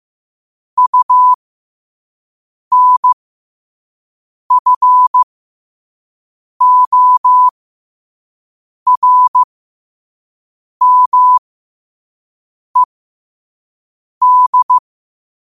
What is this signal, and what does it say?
Morse code, heard as a keyed tone, about 15 wpm, spaced as for 5 wpm, 1000 Hz. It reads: UNFORMED
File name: qotdfiles / 5wpm